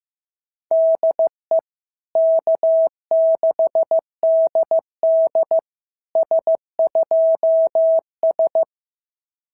Morse code, heard as a keyed tone, about 15 wpm, 650 Hz. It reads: DE K6DD S2S
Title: SOTA QSO Practice 15wpm